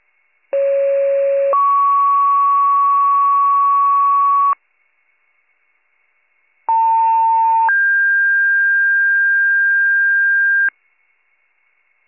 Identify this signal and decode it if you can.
Two-tone page, heard as a one-second tone, then a three-second tone, about 550 then 1120 Hz, about 900 then 1600 Hz. It is heard from 0.5 to 10.7 s.